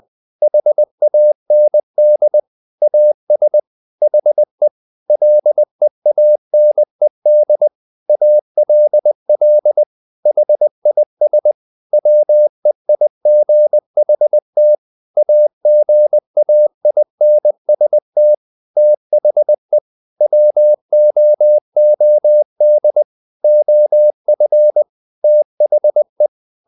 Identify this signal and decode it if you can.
Morse code, heard as a keyed tone, about 20 wpm, 600 Hz. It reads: HAND AS HE LEANED ALL HIS WEIGHT AGAINST THE WOOD OF THE
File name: the old man and the sea part 2 20wpm